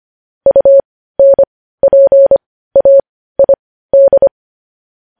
Morse code, heard as a keyed tone, about 25 wpm, spaced as for 15 wpm, 550 Hz. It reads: UNPAID
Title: News Headlines at 15